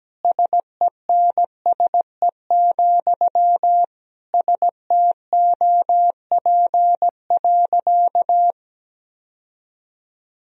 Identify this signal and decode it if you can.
Morse code, heard as a keyed tone, about 17 wpm, 700 Hz. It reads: SENSE, STOP.
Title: Sentences from Top 1000 Words - Review Part 4 - 17wpm